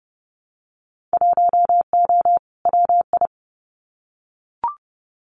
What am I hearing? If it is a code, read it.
Morse code, heard as a keyed tone, about 30 wpm, 700 Hz. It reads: 1O WI
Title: Winter Field Day Exchanges 30wpm